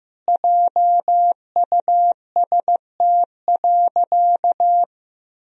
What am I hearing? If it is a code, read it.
Morse code, heard as a keyed tone, about 16 wpm, 700 Hz. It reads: JUST.
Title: Sentences from Top 1000 Words - Review Part 4 - 15wpm